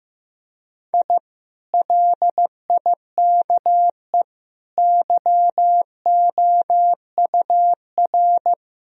Morse code, heard as a keyed tone, about 14 wpm, 700 Hz. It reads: I LIKE YOUR
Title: Sentences from Top 1000 Words - Review Part 4 - 15wpm